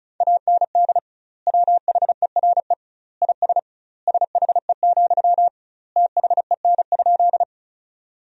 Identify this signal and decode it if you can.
Morse code, heard as a keyed tone, about 34 wpm, 700 Hz. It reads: AND WHERE IS SHE, THEN?
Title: 2x Word-Spacing - Sentences from Top 200 Words - 35wpm